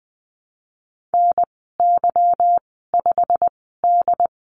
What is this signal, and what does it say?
Morse code, heard as a keyed tone, about 20 wpm, spaced as for 15 wpm, 700 Hz. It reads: NY5D
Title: 161 Lesson 26g - Letter g - Call Signs 20wpm